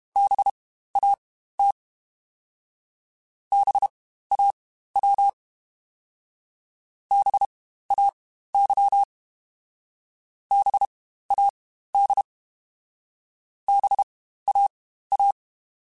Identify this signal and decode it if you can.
Morse code, heard as a keyed tone, about 32 wpm, spaced as for 15 wpm, 800 Hz. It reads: BAT BAW BAY BAD BAA
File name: Spaced Alphas / Spaced Alphas mp3s / Bs sp @30